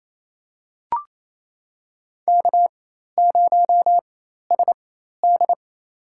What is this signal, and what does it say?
Morse code, heard as a keyed tone, about 28 wpm, spaced as for 13 wpm, 700 Hz. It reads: K0SD
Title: USA Call Signs - 4 Characters - 4x Character Spacing - 28wpm